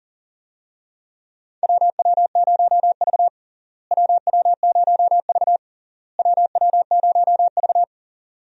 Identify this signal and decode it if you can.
Morse code, heard as a keyed tone, about 40 wpm, 700 Hz. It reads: WW0V WW0V WW0V
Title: USA Call Signs - 4 Characters - Repeated 3x - 40wpm